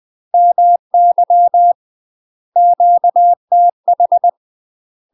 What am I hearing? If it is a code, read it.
Morse code, heard as a keyed tone, about 20 wpm, 700 Hz. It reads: MY QTH
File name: UK MASTER QSOs TRAINING / 20WPM / UK MASTER QSOs 0071